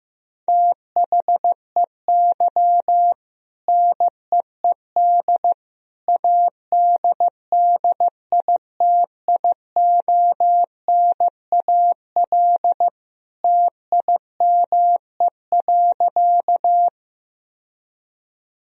Morse code, heard as a keyed tone, about 15 wpm, 700 Hz. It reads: THEY NEED ADDITIONAL TIME.
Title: Sentences from Top 1000 Words - Review Part 4 - 15wpm